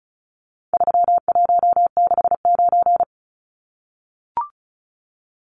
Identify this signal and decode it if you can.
Morse code, heard as a keyed tone, about 34 wpm, 700 Hz. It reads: 3169
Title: Numbers - 4-Digit 35wpm